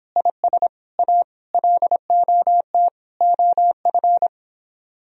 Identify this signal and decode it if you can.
Morse code, heard as a keyed tone, about 26 wpm, 700 Hz. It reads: IS A LOT OF